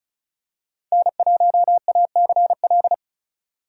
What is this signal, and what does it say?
Morse code, heard as a keyed tone, about 36 wpm, 700 Hz. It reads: N1ACL
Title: USA Call Signs - 5 Characters - 3x Character Spacing - 35wpm